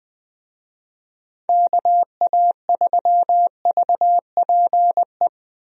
Morse code, heard as a keyed tone, about 20 wpm, 700 Hz. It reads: KA3VPE
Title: USA Call Signs - 6 Characters - Repeated 3x - 20wpm